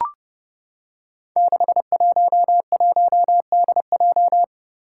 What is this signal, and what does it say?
Morse code, heard as a keyed tone, about 30 wpm, 700 Hz. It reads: 611DJ
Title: Five Challenging Letter-Number Combinations 30wpm